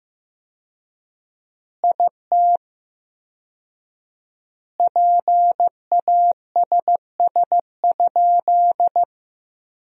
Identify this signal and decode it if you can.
Morse code, heard as a keyed tone, about 15 wpm, 700 Hz. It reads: IT PASS?
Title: 4x Word-Spacing - Sentences from Top 400 Words - 15wpm